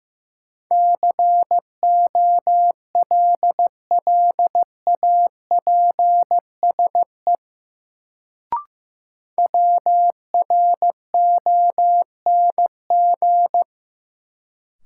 Morse code, heard as a keyed tone, about 15 wpm, 700 Hz. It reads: COLLAPSE WRONG
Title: Sets of 5 Words Encore - New Word Drill - 15wpm